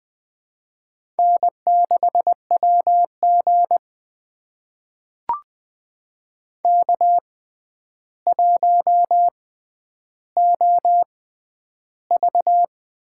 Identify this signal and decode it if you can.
Morse code, heard as a keyed tone, about 21 wpm, spaced as for 10 wpm, 700 Hz. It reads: N6WG K1OV
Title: USA Call Signs - 4 Characters - 6x Character Spacing - 20wpm